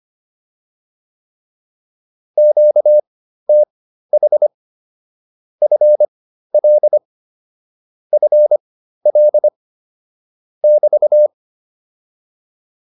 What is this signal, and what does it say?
Morse code, heard as a keyed tone, about 25 wpm, spaced as for 13 wpm, 600 Hz. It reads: QTH FL FL <BT>
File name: QSO 207 13